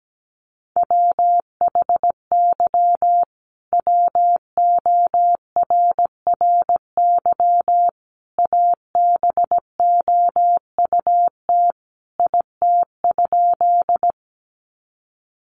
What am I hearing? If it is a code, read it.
Morse code, heard as a keyed tone, about 17 wpm, 700 Hz. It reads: WHY WORRY ABOUT IT?
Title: Sentences from Top 1100 Words 17wpm